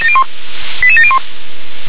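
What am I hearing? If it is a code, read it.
Five-tone selective call (ZVEI), heard as quick runs of five tones, tones of about 70 ms. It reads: 88701
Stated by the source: Received on SDR; ZVEI five-tone selective call.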